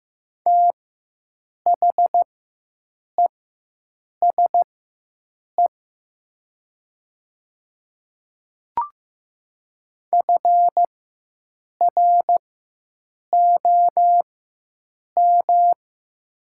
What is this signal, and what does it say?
Morse code, heard as a keyed tone, about 15 wpm, spaced as for 7 wpm, 700 Hz. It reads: THESE FROM
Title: 4x Character Spacing - Top 100 Words - 15wpm